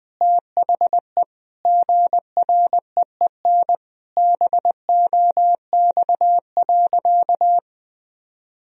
Morse code, heard as a keyed tone, about 20 wpm, 700 Hz. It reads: THE GREEN BOX.
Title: Sentences from Top 1000 Words - Review Part 2 - 20wpm